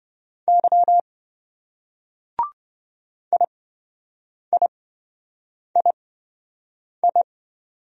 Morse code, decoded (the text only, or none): Y IIII